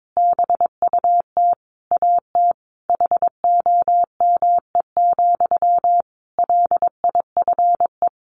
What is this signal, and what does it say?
Morse code, heard as a keyed tone, about 22 wpm, 700 Hz. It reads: BUT AT HOME, LIFE
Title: Sentences from Top 1400 Words 22wpm